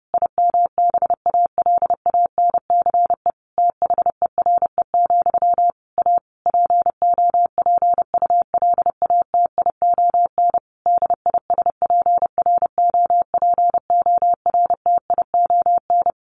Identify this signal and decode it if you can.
Morse code, heard as a keyed tone, about 30 wpm, 700 Hz. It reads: IMBALANCE THERE, A POPULATION DISPROPORTION
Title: A Sound Of Thunder 30wpm